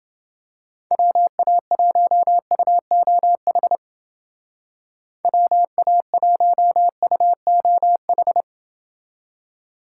Morse code, heard as a keyed tone, about 29 wpm, 700 Hz. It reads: WA1UOH WA1UOH